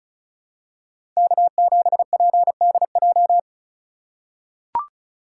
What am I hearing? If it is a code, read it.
Morse code, heard as a keyed tone, about 34 wpm, 700 Hz. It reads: K7PDJ
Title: USA Call Signs - 2x Character Spacing - 35wpm